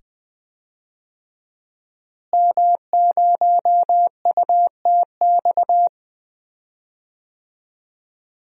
Morse code, heard as a keyed tone, about 20 wpm, 700 Hz. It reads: M0UTX
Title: DXCC-100 Call Signs - 5 Characters - Repeated 2x - 20wpm